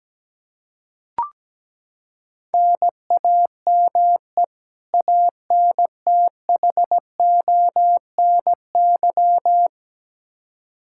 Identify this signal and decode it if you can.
Morse code, heard as a keyed tone, about 17 wpm, 700 Hz. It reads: NAME ANTHONY
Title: K1USN SST QSO Practice 17wpm